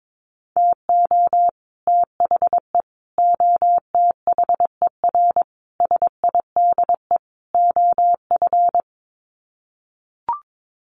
Morse code, heard as a keyed tone, about 22 wpm, 700 Hz. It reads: TO THE OTHER SIDE OF
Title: Sets of 5 Words Encore 22wpm